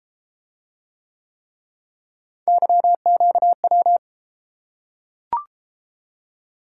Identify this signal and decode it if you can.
Morse code, heard as a keyed tone, about 32 wpm, 700 Hz. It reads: YQW